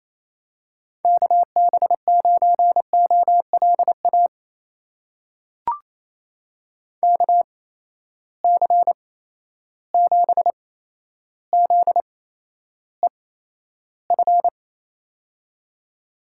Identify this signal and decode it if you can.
Morse code, heard as a keyed tone, about 27 wpm, spaced as for 12 wpm, 700 Hz. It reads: KB9OLA KC7ZEF
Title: USA Call Signs - 6 Characters - 8x Character Spacing - 28wpm